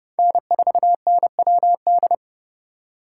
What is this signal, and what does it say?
Morse code, heard as a keyed tone, about 28 wpm, 700 Hz. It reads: N4NWD